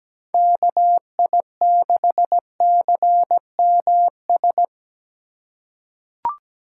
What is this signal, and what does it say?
Morse code, heard as a keyed tone, about 17 wpm, 700 Hz. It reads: KI6CMS
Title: USA Call Signs - 3x Character Spacing - 17wpm